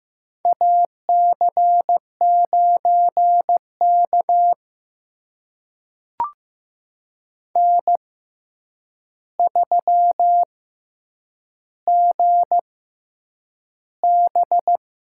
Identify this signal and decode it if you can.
Morse code, heard as a keyed tone, about 15 wpm, spaced as for 8 wpm, 700 Hz. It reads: AC9K N3GB
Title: USA Call Signs - 4 Characters - 6x Character Spacing - 15wpm